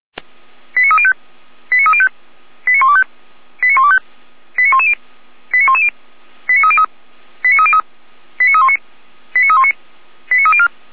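Five-tone selective call (ZVEI), heard as quick runs of five tones, tones of about 70 ms. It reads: 89305 89305 89125 89125 89110 89110 89303 89303 89319 89319 89304
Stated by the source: ZVEI five-tone call received by SDR, logged by Feuerwehr Orlamünde (f85360).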